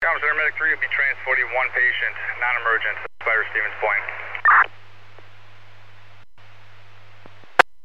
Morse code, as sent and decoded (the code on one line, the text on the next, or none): none